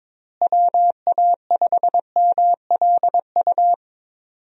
Morse code, heard as a keyed tone, about 21 wpm, 700 Hz. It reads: WA5MLU
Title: USA Call Signs - 6 Characters - Repeated 3x - 22wpm